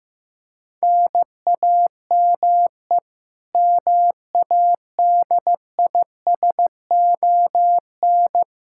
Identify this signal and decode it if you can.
Morse code, heard as a keyed tone, about 15 wpm, 700 Hz. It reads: NAME MADISON
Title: Practice QSO - Exchange 1 - Beginning - 15wpm